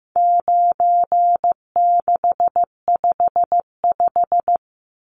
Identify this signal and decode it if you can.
Morse code, heard as a keyed tone, about 15 wpm, 700 Hz. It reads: 9655
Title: Numbers - 4-Digit 15wpm